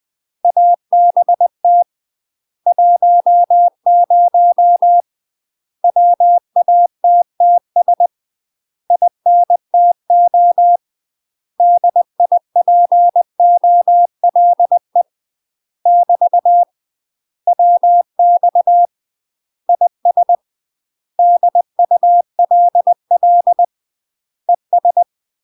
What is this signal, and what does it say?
Morse code, heard as a keyed tone, about 20 wpm, 700 Hz. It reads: ABT 10 WATTS INTO DIPOLE <BT> WX IS DULL ES
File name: UK MASTER QSOs TRAINING / 20WPM / UK MASTER QSOs 0010